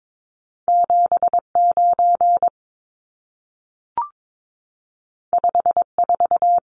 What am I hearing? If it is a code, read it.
Morse code, heard as a keyed tone, about 22 wpm, 700 Hz. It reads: 79 54
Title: Numbers - 2-Digit 22wpm